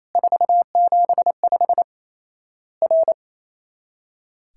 Morse code, heard as a keyed tone, about 28 wpm, 700 Hz. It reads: 475 R
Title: Worked All Europe DX Contest - QTC Exchange - 28wpm